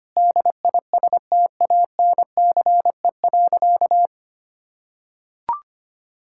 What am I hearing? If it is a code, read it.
Morse code, heard as a keyed tone, about 25 wpm, 700 Hz. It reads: DISTANCE.
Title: Sentences from Top 1600 Words 25wpm